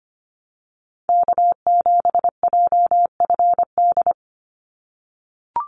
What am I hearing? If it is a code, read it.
Morse code, heard as a keyed tone, about 25 wpm, 700 Hz. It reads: K7JFD
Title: NA SOTA Chaser Callsigns 25wpm